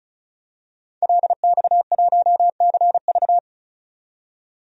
Morse code, heard as a keyed tone, about 36 wpm, 700 Hz. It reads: LX1CV
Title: DXCC-100 Call Signs - 5 Characters - 35wpm